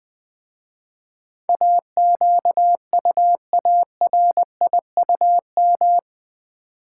Morse code, heard as a keyed tone, about 20 wpm, 700 Hz. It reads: AQUARIUM
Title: Easy Words with Hints - 20wpm